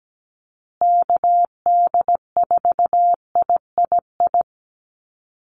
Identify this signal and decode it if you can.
Morse code, heard as a keyed tone, about 17 wpm, 700 Hz. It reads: KD4III
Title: USA Call Signs Repeated 2x 17wpm